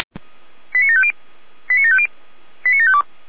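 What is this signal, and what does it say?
Five-tone selective call (ZVEI), heard as quick runs of five tones, tones of about 70 ms. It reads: 89750 89750 89752
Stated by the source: ZVEI five-tone call 89584 received via SDR.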